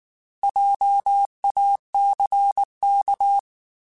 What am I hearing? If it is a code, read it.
Morse code, heard as a keyed tone, about 19 wpm, 800 Hz. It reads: JACK
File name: Alphabetical / Alphabetical mp3s / Js @18